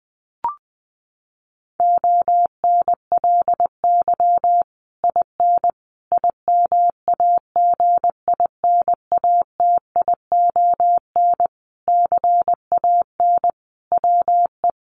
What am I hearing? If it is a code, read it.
Morse code, heard as a keyed tone, about 20 wpm, 700 Hz. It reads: ONLY IN IMAGINATION CAN WE
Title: Sentences from Top 1300 Words 20wpm